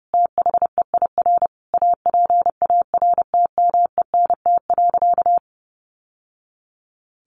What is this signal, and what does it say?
Morse code, heard as a keyed tone, about 30 wpm, 700 Hz. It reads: THEIR APARTMENT.
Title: Sentences from Top 900 Words 30wpm